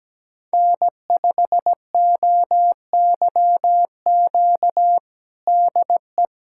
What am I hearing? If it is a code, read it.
Morse code, heard as a keyed tone, about 17 wpm, 700 Hz. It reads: N5OYQ DE
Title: USA Callsign DE Callsign 17wpm